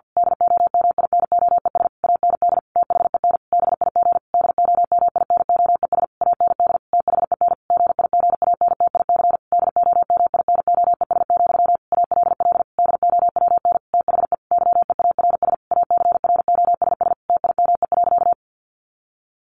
Morse code, heard as a keyed tone, about 50 wpm, 700 Hz. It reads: DOMINOES AND THEN BIG DOMINOES AND THEN GIGANTIC DOMINOES, ALL DOWN THE YEARS ACROSS TIME.
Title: A Sound Of Thunder 50wpm